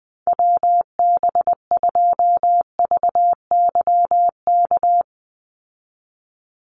Morse code, heard as a keyed tone, about 19 wpm, 700 Hz. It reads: WB2VYK
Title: Call Signs Repeated 2x - 20wpm